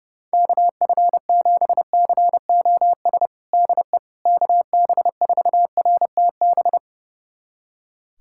Morse code, heard as a keyed tone, about 29 wpm, 700 Hz. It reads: KF7COS DE KB4RTB
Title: USA Callsign DE Callsign - 6 Characters - 30wpm